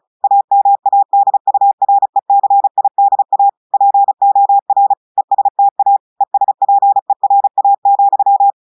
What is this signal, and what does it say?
Morse code, heard as a keyed tone, about 36 wpm, 800 Hz. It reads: AMADURECIDA POR ESTA ESPERA,
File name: audios / portugais / portugais texte en clair 1 vitesse 35 mots minutes